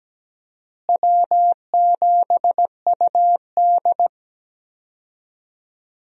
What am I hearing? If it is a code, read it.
Morse code, heard as a keyed tone, about 17 wpm, 700 Hz. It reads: W7UD
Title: USA Call Signs - 4 Characters - Repeated 2x - 17wpm